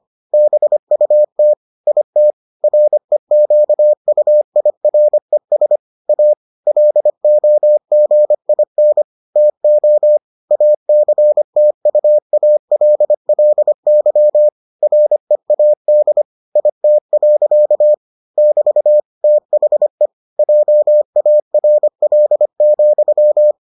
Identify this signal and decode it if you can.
Morse code, heard as a keyed tone, about 25 wpm, 600 Hz. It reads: BUT IT REQUIRES A LOGIN TO ACTUALLY READ IT. <BT> THE JARL,